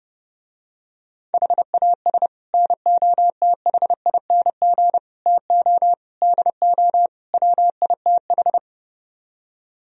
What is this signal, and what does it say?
Morse code, heard as a keyed tone, about 30 wpm, 700 Hz. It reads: HAS NOTHING TO DO WITH